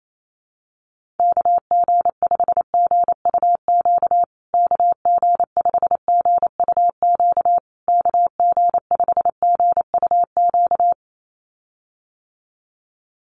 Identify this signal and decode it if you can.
Morse code, heard as a keyed tone, about 28 wpm, 690 Hz. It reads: KG5GUQ KG5GUQ KG5GUQ
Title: USA Call Signs Repeated 3x 28wpm